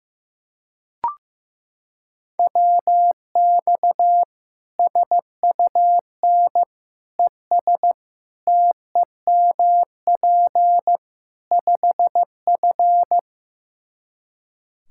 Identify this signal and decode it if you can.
Morse code, heard as a keyed tone, about 15 wpm, 700 Hz. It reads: WX SUN ES TEMP 5F
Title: Practice QSO - Exchange 2 - Info - 15wpm